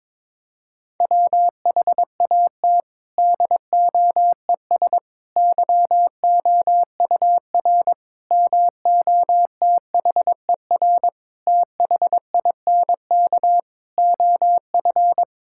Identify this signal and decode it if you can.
Morse code, heard as a keyed tone, about 22 wpm, 700 Hz. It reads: WHAT DOES YOUR MOTHER THINK OF
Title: Sentences from Top 1000 Words - Review Part 5 - 22wpm